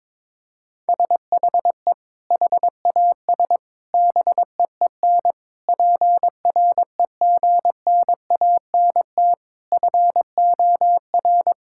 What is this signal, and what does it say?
Morse code, heard as a keyed tone, about 22 wpm, 700 Hz. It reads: SHE HAS BEEN PREGNANT FOR
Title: Sentences from Top 1000 Words - Review Part 6 - 22wpm